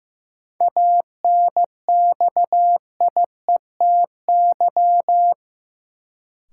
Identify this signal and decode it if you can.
Morse code, heard as a keyed tone, about 16 wpm, 700 Hz. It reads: ANXIETY
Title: Sentences from Top 1600 Words - New Word Drill Alt - 15wpm